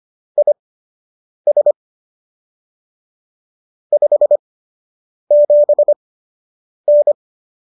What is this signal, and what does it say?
Morse code, heard as a keyed tone, about 25 wpm, spaced as for 8 wpm, 590 Hz. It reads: IS 57N